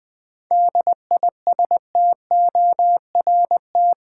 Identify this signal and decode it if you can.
Morse code, heard as a keyed tone, about 20 wpm, 700 Hz. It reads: DISTORT
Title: 7 Letter Words - First 3 Letters Given - 20wpm